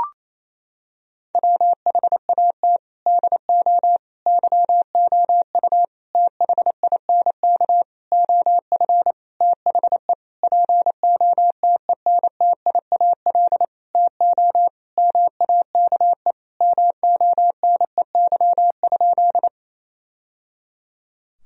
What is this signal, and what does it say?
Morse code, heard as a keyed tone, about 27 wpm, 700 Hz. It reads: WHAT DO YOU THINK OF THE POTENTIAL TO MAKE MONEY?
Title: Sentences from Top 1000 Words - Review Part 4 - 28wpm